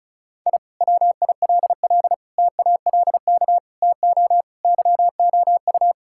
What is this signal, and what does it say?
Morse code, heard as a keyed tone, about 36 wpm, 700 Hz. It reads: I WILL TALK TO YOU
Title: Sentences from Top 1000 Words - Review Part 6 - 35wpm